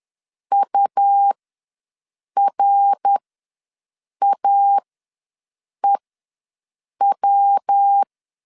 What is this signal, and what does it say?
Morse code, heard as a keyed tone, about 10 wpm, spaced as for 6 wpm, 800 Hz. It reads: URAEW
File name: LEZIONI / 8 YJ45 / 8 Y ws 10wpm 3952